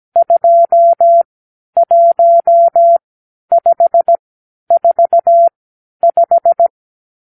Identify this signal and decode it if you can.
Morse code, heard as a keyed tone, about 17 wpm, spaced as for 11 wpm, 680 Hz. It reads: 21545
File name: k7qo / Lesson 50 (1-6)